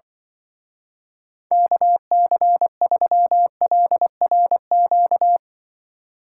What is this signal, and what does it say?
Morse code, heard as a keyed tone, about 24 wpm, 700 Hz. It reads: KC3LRQ
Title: Speed Racing - USA 2x3 Call Signs 20wpm